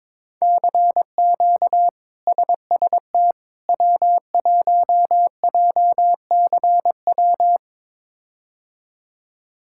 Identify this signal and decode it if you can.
Morse code, heard as a keyed tone, about 22 wpm, 700 Hz. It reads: CQ SST W1JCW
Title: K1USN SST QSO Practice - 22wpm